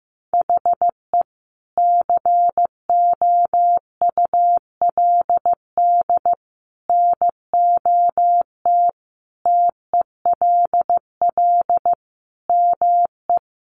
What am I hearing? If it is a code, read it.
Morse code, heard as a keyed tone, about 15 wpm, 700 Hz. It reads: HE COULD NOT TELL ME
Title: Sentences from Top 600 Words 15wpm